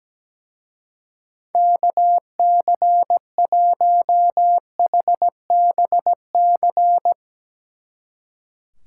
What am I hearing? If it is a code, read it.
Morse code, heard as a keyed tone, about 17 wpm, 700 Hz. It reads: KC1HBC
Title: Speed Racing - USA 2x3 Call Signs 17wpm